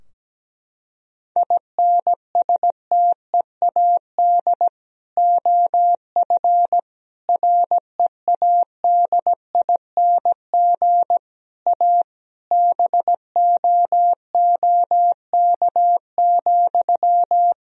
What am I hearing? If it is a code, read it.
Morse code, heard as a keyed tone, about 17 wpm, 700 Hz. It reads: INSTEAD OF READING A BOOK,
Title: Sentences from Top 1000 Words - Review Part 4 - 17wpm